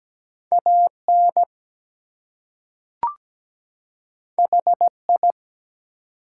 Morse code, heard as a keyed tone, about 17 wpm, 700 Hz. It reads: AN HI
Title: Two Letter Words - 17wpm